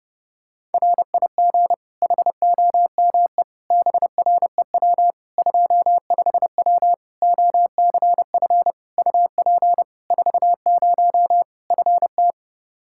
Morse code, heard as a keyed tone, about 32 wpm, 700 Hz. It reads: RIG HOME BREW 25W OCF UP 40 FT
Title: Practice QSO - Exchange 2 - Info - 30wpm